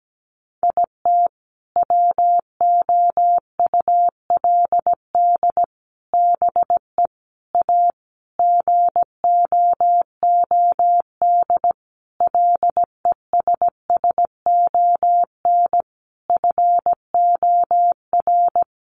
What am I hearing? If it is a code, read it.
Morse code, heard as a keyed tone, about 17 wpm, 700 Hz. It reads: IT WOULD BE A GOOD LESSON FOR
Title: Sentences from Top 1200 Words 17wpm